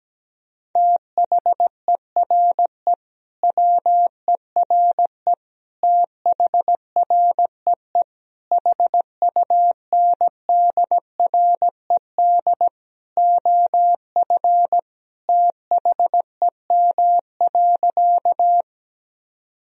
Morse code, heard as a keyed tone, about 17 wpm, 700 Hz. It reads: THERE WERE THREE HUNDRED OF THEM.
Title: Sentences from Top 1000 Words - Review Part 2 - 17wpm